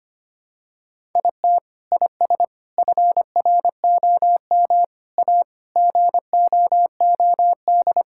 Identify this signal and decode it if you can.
Morse code, heard as a keyed tone, about 25 wpm, 700 Hz. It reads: IT IS FROM A GOOD
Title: Sentences from Top 1000 Words - Review Part 5 - 25wpm